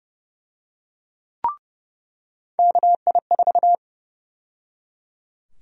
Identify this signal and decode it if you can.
Morse code, heard as a keyed tone, about 29 wpm, 700 Hz. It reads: KI4
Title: USA Call Sign Prefixes - 30wpm